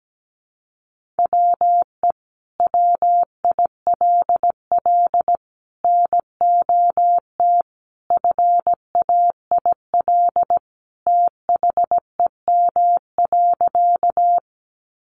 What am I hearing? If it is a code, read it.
Morse code, heard as a keyed tone, about 17 wpm, 700 Hz. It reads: WE WILL NOT FAIL THEM.
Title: Sentences from Top 1100 Words 17wpm